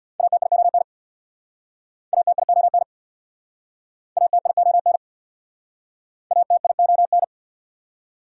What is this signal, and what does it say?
Morse code, heard as a keyed tone, about 64 wpm, 690 Hz. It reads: ATION ATION ATION ATION